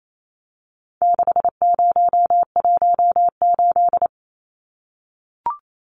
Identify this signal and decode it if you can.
Morse code, heard as a keyed tone, about 28 wpm, 700 Hz. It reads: 6018
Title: Numbers - 4-Digit 28wpm